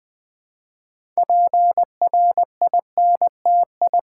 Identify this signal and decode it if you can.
Morse code, heard as a keyed tone, about 20 wpm, 700 Hz. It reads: PRINTI
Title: Build A Word - 3 Letters At A Time - 8-Letter Words - 20wpm